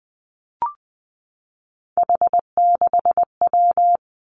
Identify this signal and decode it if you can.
Morse code, heard as a keyed tone, about 20 wpm, 700 Hz. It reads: H6W
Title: Three Challenging Letter-Number Combinations 20wpm